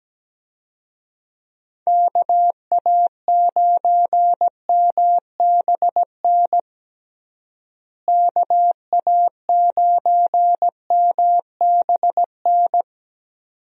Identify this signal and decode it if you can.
Morse code, heard as a keyed tone, about 17 wpm, 700 Hz. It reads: KA9MBN KA9MBN
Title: USA Call Signs - 6 Characters - Repeated 2x - 17wpm